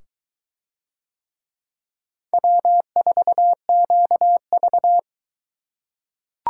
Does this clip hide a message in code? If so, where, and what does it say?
Yes: Morse code from 2.3 to 5.0 s, heard as a keyed tone, about 23 wpm, 700 Hz. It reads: W4QV